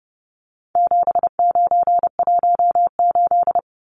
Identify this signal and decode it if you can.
Morse code, heard as a keyed tone, about 30 wpm, 700 Hz. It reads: 7918
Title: Numbers - 4-Digit 30wpm